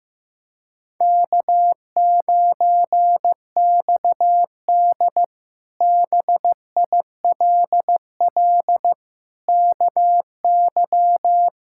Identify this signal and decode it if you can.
Morse code, heard as a keyed tone, about 15 wpm, 700 Hz. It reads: K9XD BILL KY
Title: K1USN SST QSO Practice - 15wpm